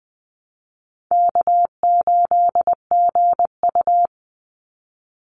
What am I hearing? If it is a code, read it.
Morse code, heard as a keyed tone, about 20 wpm, 700 Hz. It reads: K8GU
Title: Easy Call Signs Repeated 3x - 20wpm